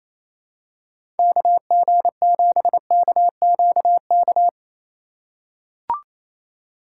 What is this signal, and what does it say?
Morse code, heard as a keyed tone, about 28 wpm, 700 Hz. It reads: KG7KQK
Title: USA Call Signs - 3x Character Spacing - 28wpm